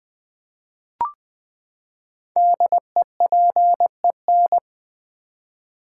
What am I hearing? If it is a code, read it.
Morse code, heard as a keyed tone, about 21 wpm, 700 Hz. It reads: DEPEN
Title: Build A Word - 5 Letters At A Time - 10-Letter Words - 20wpm